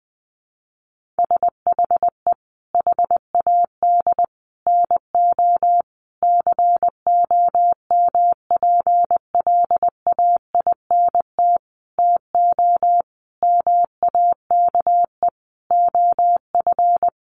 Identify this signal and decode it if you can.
Morse code, heard as a keyed tone, about 20 wpm, 700 Hz. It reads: SHE HAD NO COMPLAINT TO MAKE OF
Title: Sentences from Top 1400 Words 20wpm